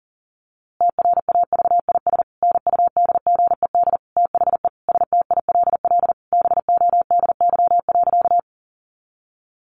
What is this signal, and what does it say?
Morse code, heard as a keyed tone, about 40 wpm, 700 Hz. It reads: TRAVIS NUDGED THE STILL BODY.
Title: A Sound Of Thunder 40wpm